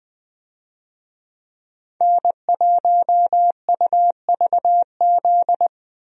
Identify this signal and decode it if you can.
Morse code, heard as a keyed tone, about 20 wpm, 700 Hz. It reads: N1UVZ
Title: USA Call Signs - 5 Characters - 20wpm